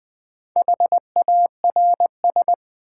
Morse code, heard as a keyed tone, about 20 wpm, 700 Hz. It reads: HARS
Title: Build A Word - 1 Letter At A Time - 3 to 5-Letter Words - 20wpm